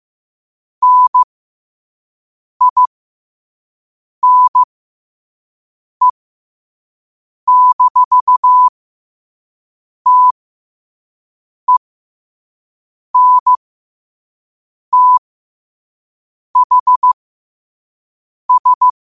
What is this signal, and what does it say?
Morse code, heard as a keyed tone, about 15 wpm, spaced as for 5 wpm, 1000 Hz. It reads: NINE-TENTHS
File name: qotdfiles / 5wpm